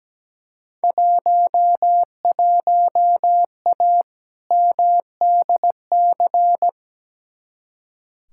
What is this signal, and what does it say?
Morse code, heard as a keyed tone, about 17 wpm, 700 Hz. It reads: 11A MDC
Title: Field Day Exchanges Encore - 17wpm